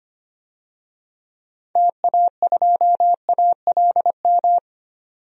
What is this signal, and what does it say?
Morse code, heard as a keyed tone, about 25 wpm, 700 Hz. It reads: TA2ALM
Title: DXCC-100 Call Signs - 6 Characters - 25wpm